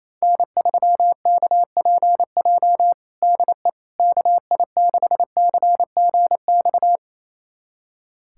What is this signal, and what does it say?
Morse code, heard as a keyed tone, about 27 wpm, 700 Hz. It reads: N3KPJ DE KI6CGX
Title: USA Callsign DE Callsign 28wpm